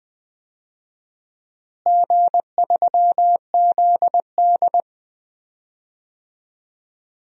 Morse code, heard as a keyed tone, about 21 wpm, 700 Hz. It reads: G3ZD
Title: DXCC-100 Call Signs - 4 Characters - Repeated 3x - 20wpm